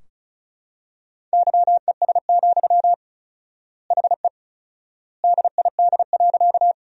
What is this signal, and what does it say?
Morse code, heard as a keyed tone, about 36 wpm, 700 Hz. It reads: YES, HE DID.
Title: 4x Word-Spacing - Sentences from Top 500 Words - 35wpm